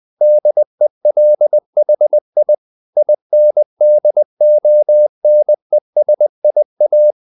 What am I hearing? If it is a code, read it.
Morse code, heard as a keyed tone, about 20 wpm, 600 Hz. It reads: DELHI INDONESIA
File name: CW / Practice / NationalCapitals20